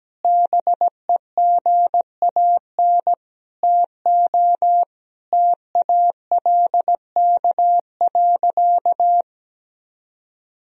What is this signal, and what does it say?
Morse code, heard as a keyed tone, about 17 wpm, 700 Hz. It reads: BEGAN TO TALK.
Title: Sentences from Top 400 Words and Abbreviations 17wpm